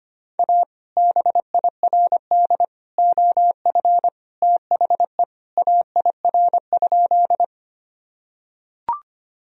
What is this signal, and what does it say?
Morse code, heard as a keyed tone, about 25 wpm, 700 Hz. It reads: A BIRD OF THE AIR?
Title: Sentences from Top 1000 Words - Review Part 2 - 25wpm